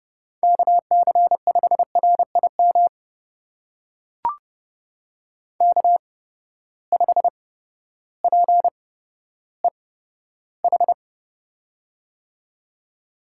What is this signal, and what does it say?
Morse code, heard as a keyed tone, about 31 wpm, spaced as for 14 wpm, 700 Hz. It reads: KC5RIM K5PEH
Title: USA Call Signs - 8x Character Spacing - 30wpm